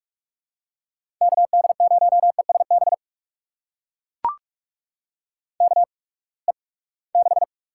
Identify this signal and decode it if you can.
Morse code, heard as a keyed tone, about 44 wpm, spaced as for 25 wpm, 700 Hz. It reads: KD0ESB KE6
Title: USA Call Signs - 6 Characters - 8x Character Spacing - 45wpm